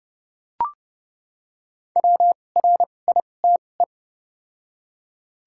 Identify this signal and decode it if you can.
Morse code, heard as a keyed tone, about 28 wpm, spaced as for 21 wpm, 690 Hz. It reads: WRITE
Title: 2x Character Spacing - Top 500 Words - 30wpm